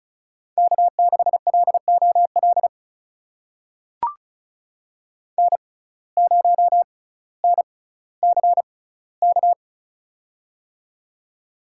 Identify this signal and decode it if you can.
Morse code, heard as a keyed tone, about 36 wpm, spaced as for 18 wpm, 700 Hz. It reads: K6LOL N0NCK
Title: USA Call Signs - 5 Characters - 6x Character Spacing - 35wpm